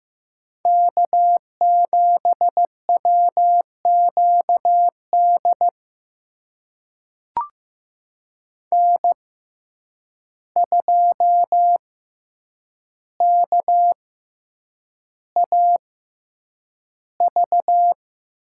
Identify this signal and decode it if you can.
Morse code, heard as a keyed tone, about 15 wpm, spaced as for 8 wpm, 700 Hz. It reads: K7WQD N2KAV
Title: USA Call Signs - 5 Characters - 6x Character Spacing - 15wpm